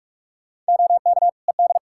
Morse code, heard as a keyed tone, about 46 wpm, 700 Hz. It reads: OK ED
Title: K1USN SST QSO Practice 45wpm